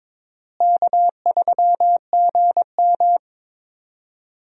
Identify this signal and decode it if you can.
Morse code, heard as a keyed tone, about 22 wpm, 700 Hz. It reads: K3GM
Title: USA Call Signs - 4 Characters - Repeated 2x - 22wpm